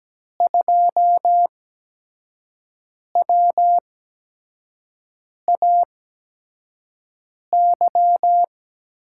Morse code, heard as a keyed tone, about 17 wpm, spaced as for 5 wpm, 700 Hz. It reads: 2WAY
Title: USA Call Signs - 8x Character Spacing - 17wpm